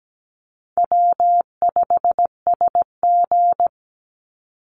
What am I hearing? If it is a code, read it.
Morse code, heard as a keyed tone, about 17 wpm, 700 Hz. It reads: W5SG
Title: Easy Call Signs - 17wpm